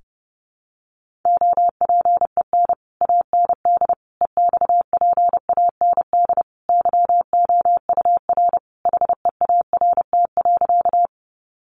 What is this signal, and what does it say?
Morse code, heard as a keyed tone, about 28 wpm, 700 Hz. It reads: OPEN AND EXPAND YOUR HEART.
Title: Sentences from Top 1400 Words 30wpm